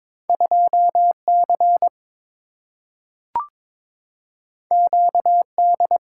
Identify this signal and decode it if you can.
Morse code, heard as a keyed tone, about 22 wpm, 700 Hz. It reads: 2C QD
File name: practice / 22wpm / Two Letter-Number Combinations - 22wpm